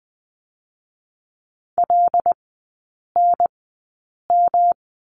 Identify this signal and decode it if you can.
Morse code, heard as a keyed tone, about 20 wpm, spaced as for 8 wpm, 700 Hz. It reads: LNM
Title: 127 Lesson 21e - Number 6 - Three Characters 20wpm